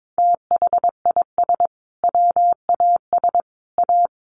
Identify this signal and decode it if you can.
Morse code, heard as a keyed tone, about 22 wpm, 700 Hz. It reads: THIS WAS A
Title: Sentences from Top 1400 Words 22wpm